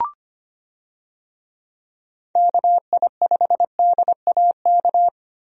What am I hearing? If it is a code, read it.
Morse code, heard as a keyed tone, about 25 wpm, 700 Hz. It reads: KI5DAK
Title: USA Call Signs - 6 Characters - Repeated 3x - 25wpm